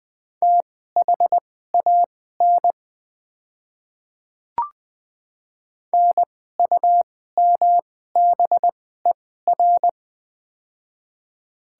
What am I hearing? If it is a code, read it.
Morse code, heard as a keyed tone, about 21 wpm, spaced as for 15 wpm, 690 Hz. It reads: THAN NUMBER
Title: 2x Character Spacing - Top 100 Words - 20wpm